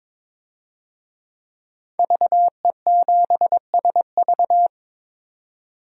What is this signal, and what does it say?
Morse code, heard as a keyed tone, about 22 wpm, 700 Hz. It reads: VE7SV
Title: DXCC-100 Call Signs - 5 Characters - 22wpm